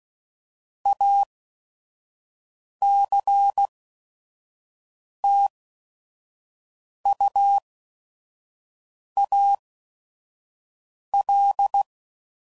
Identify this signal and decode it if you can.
Morse code, heard as a keyed tone, about 16 wpm, spaced as for 5 wpm, 800 Hz. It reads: ACTUAL
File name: CW Code Groups @5WPM / Technical Subjects / XC